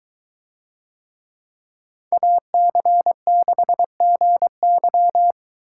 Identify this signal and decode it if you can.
Morse code, heard as a keyed tone, about 22 wpm, 700 Hz. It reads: AC6GY